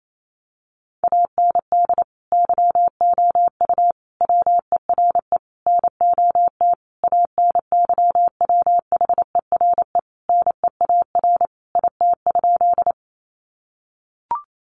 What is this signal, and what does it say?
Morse code, heard as a keyed tone, about 28 wpm, 700 Hz. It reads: AND YOU WERE NOT ANYWHERE NEAR IT?
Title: Sentences from Top 1400 Words 28wpm